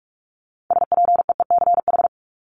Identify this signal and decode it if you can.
Morse code, heard as a keyed tone, about 44 wpm, 700 Hz. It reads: SPEECH
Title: Top 1751-2000 Words 45wpm